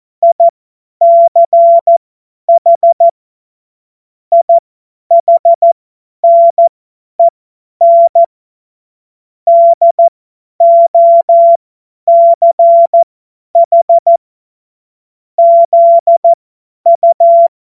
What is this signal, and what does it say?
Morse code, heard as a keyed tone, about 14 wpm, spaced as for 10 wpm, 680 Hz. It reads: ICH IHNEN DOCH ZU
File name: Morsekurs / Witze / Witz 23. August 2022 50 F70